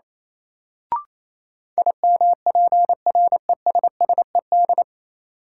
Morse code, heard as a keyed tone, about 28 wpm, 700 Hz. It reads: IMPRESSED
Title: Sentences from Top 1600 Words - New Word Drill - 28wpm